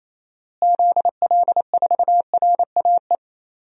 Morse code, heard as a keyed tone, about 28 wpm, 700 Hz. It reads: ZL4RAE
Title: DXCC-100 Call Signs - 6 Characters - Repeated 3x - 28wpm